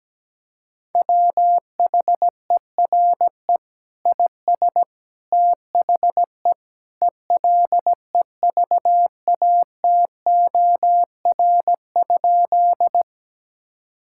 Morse code, heard as a keyed tone, about 17 wpm, 700 Hz. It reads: WHERE IS THE ELEVATOR?
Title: Sentences from Top 1600 Words 17wpm